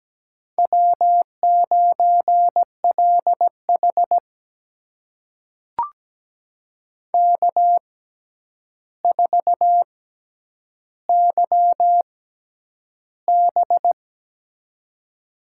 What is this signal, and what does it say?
Morse code, heard as a keyed tone, about 17 wpm, spaced as for 9 wpm, 700 Hz. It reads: W9LH K4YB
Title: USA Call Signs - 4 Characters - 6x Character Spacing - 17wpm